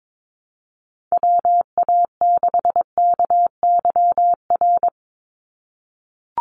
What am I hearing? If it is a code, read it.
Morse code, heard as a keyed tone, about 22 wpm, 700 Hz. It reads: WA6KYR
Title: NA SOTA Chaser Callsigns 22wpm